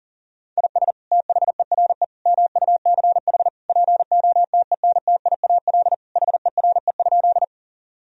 Morse code, heard as a keyed tone, about 37 wpm, 700 Hz. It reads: IS THERE MUCH POTENTIAL HERE?
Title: Sentences from Top 1000 Words - Review Part 4 - 40wpm